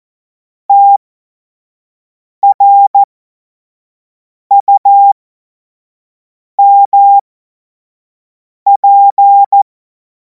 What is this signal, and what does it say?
Morse code, heard as a keyed tone, about 14 wpm, spaced as for 5 wpm, 800 Hz. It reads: TRUMP
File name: Morsekurs / Witze / Witz 16. Juni 2026 25 F70